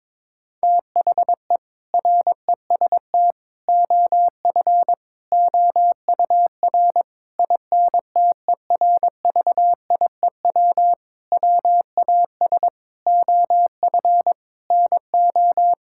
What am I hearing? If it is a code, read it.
Morse code, heard as a keyed tone, about 22 wpm, 700 Hz. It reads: THE REST OF OUR INTERVIEW WAS OF NO
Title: Sentences from Top 1600 Words 22wpm